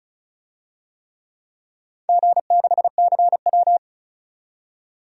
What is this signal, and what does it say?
Morse code, heard as a keyed tone, about 36 wpm, 700 Hz. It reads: G6CW
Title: DXCC-100 Call Signs - 4 Characters - 35wpm